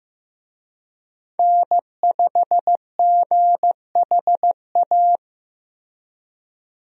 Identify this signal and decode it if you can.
Morse code, heard as a keyed tone, about 15 wpm, 700 Hz. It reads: N5GHA
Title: USA Call Signs - 5 Characters - Repeated 2x - 15wpm